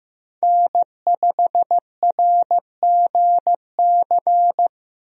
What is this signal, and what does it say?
Morse code, heard as a keyed tone, about 15 wpm, 700 Hz. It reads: N5RGC
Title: USA Call Signs - 5 Characters - 15wpm